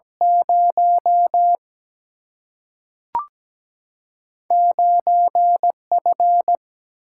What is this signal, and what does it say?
Morse code, heard as a keyed tone, about 17 wpm, 700 Hz. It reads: 0 9F
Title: Single-Letter Single-Number 17wpm